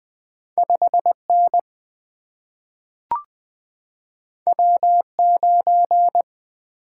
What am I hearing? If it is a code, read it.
Morse code, heard as a keyed tone, about 20 wpm, 700 Hz. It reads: 5N W9
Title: Two Letter-Number Combinations - 20wpm